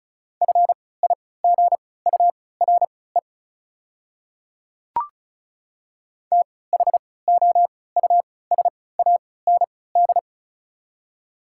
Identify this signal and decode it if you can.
Morse code, heard as a keyed tone, about 34 wpm, spaced as for 20 wpm, 700 Hz. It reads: FIGURE THOUSAND
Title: 3x Character Spacing - Top 500 Words - 35wpm